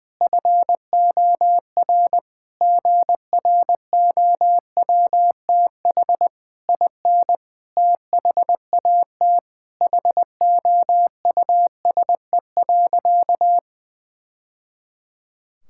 Morse code, heard as a keyed tone, about 21 wpm, 700 Hz. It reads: FOR GROWTH IN THAT HOUSE.
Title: Sentences from Top 1000 Words - Review Part 5 - 20wpm